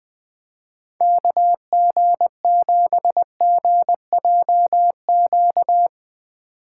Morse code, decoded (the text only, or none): KG7GJQ